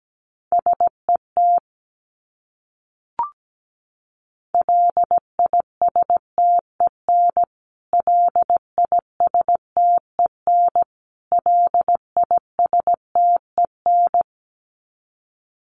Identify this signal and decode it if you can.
Morse code, heard as a keyed tone, about 17 wpm, 700 Hz. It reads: SET LISTEN LISTEN LISTEN
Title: Top 500 Words Repeated 17wpm